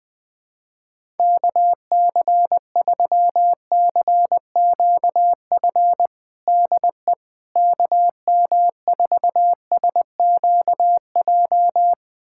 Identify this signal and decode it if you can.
Morse code, heard as a keyed tone, about 20 wpm, 700 Hz. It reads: KC3CQF DE KM4SQJ
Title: USA Callsign DE Callsign - 6 Characters - 20wpm